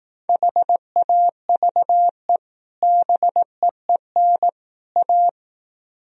Morse code, heard as a keyed tone, about 18 wpm, 700 Hz. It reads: HAVE BEEN A
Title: Speed-Racing - Sets of 3 Words 15wpm